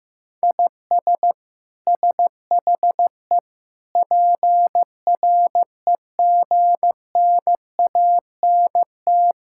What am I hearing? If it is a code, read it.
Morse code, heard as a keyed tone, about 16 wpm, 700 Hz. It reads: IS SHE PREGNANT
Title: Sentences from Top 1000 Words - Review Part 6 - 15wpm